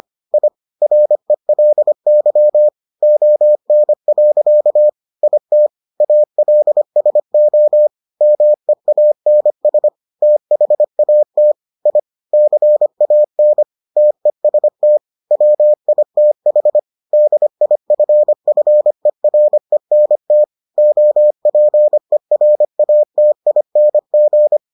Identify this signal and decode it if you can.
Morse code, heard as a keyed tone, about 25 wpm, 600 Hz. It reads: I RELY ON. IT ALSO MEANS THAT I CAN TEST WITH DIFFERENT OPERATING